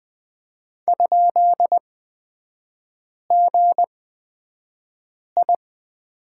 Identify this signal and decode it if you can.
Morse code, heard as a keyed tone, about 20 wpm, spaced as for 5 wpm, 700 Hz. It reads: ?GI
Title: 243 Lesson 39e - Letter z - Three Characters 20wpm